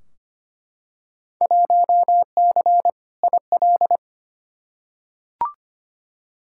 Morse code, heard as a keyed tone, about 25 wpm, 700 Hz. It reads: 1C IL
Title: Field Day Exchanges - 25wpm